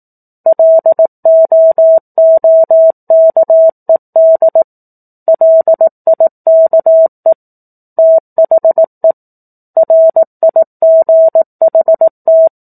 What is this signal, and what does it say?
Morse code, heard as a keyed tone, about 18 wpm, 650 Hz. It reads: LOOKED LIKE THE RIGHT
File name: 04-Chapter-04 18 WPM